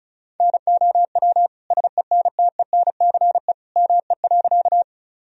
Morse code, heard as a keyed tone, about 36 wpm, 700 Hz. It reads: NOW SENTENCE ME.
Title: Sentences from Top 1000 Words - Review Part 1 - 35wpm